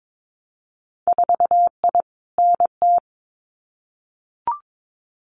Morse code, heard as a keyed tone, about 22 wpm, 700 Hz. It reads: 4I NT
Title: Winter Field Day Exchanges Encore 22wpm